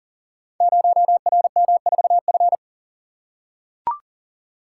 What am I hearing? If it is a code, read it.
Morse code, heard as a keyed tone, about 42 wpm, 700 Hz. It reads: 0RM4F
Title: Code Groups - 5 Letter:Numbers - 40wpm